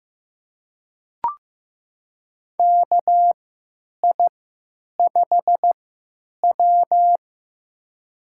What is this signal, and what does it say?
Morse code, heard as a keyed tone, about 15 wpm, spaced as for 9 wpm, 700 Hz. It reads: KI5W
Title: USA Call Signs - 3x Character Spacing - 15wpm